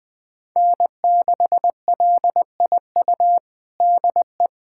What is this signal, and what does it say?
Morse code, heard as a keyed tone, about 20 wpm, 700 Hz. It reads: N6LIU DE
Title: USA Callsign DE Callsign - 5 Characters - 20wpm